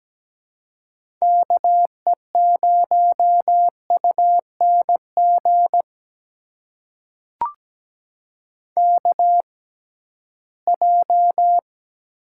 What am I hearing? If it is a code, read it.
Morse code, heard as a keyed tone, about 17 wpm, spaced as for 13 wpm, 700 Hz. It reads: KE0UNG KJ
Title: USA Call Signs - 6 Characters - 6x Character Spacing - 17wpm